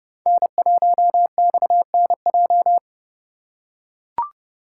Morse code, heard as a keyed tone, about 31 wpm, 700 Hz. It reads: N1XNJ
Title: USA Call Signs - 5 Characters - 2x Character Spacing - 30wpm